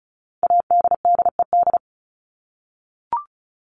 Morse code, heard as a keyed tone, about 34 wpm, 700 Hz. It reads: ADDED
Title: ED Suffix - Top 100 Words with Base Words 35wpm